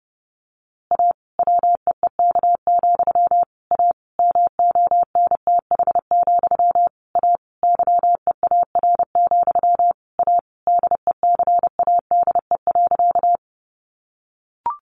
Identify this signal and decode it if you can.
Morse code, heard as a keyed tone, about 28 wpm, 700 Hz. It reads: A WEEK, A MONTH, A YEAR, A DECADE.
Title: A Sound Of Thunder 30wpm